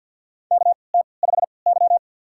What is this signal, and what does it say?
Morse code, heard as a keyed tone, about 53 wpm, spaced as for 29 wpm, 700 Hz. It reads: KT5Y
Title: USA Call Signs - 4 Characters - 3x Character Spacing - 50wpm